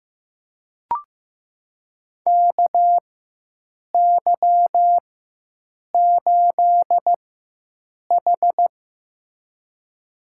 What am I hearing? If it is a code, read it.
Morse code, heard as a keyed tone, about 15 wpm, spaced as for 7 wpm, 700 Hz. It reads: KY8H
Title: USA Call Signs - 4x Character Spacing - 15wpm